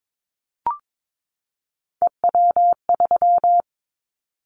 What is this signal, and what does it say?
Morse code, heard as a keyed tone, about 22 wpm, 700 Hz. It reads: EW3
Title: Callsign Prefixes for Top 50 DXCC - 22wpm